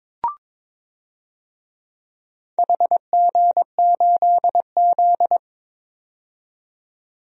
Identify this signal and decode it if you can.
Morse code, heard as a keyed tone, about 22 wpm, 700 Hz. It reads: HG8Z
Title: DXCC-100 Call Signs - 4 Characters - 22wpm